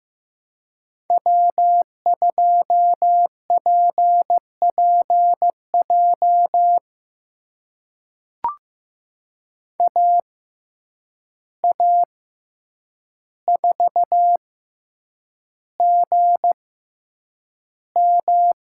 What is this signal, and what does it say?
Morse code, heard as a keyed tone, about 15 wpm, spaced as for 8 wpm, 700 Hz. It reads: W2PPJ AA4GM
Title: USA Call Signs - 5 Characters - 6x Character Spacing - 15wpm